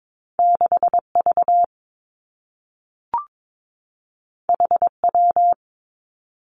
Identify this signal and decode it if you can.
Morse code, heard as a keyed tone, about 22 wpm, 700 Hz. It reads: 6 V H W
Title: Two Challenging Letter-Number Combinations 22wpm